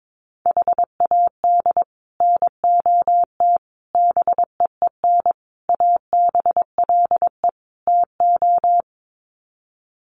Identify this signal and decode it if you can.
Morse code, heard as a keyed tone, about 22 wpm, 700 Hz. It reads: HAD NOT BEEN ABLE TO
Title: Sets of 5 Words Encore 22wpm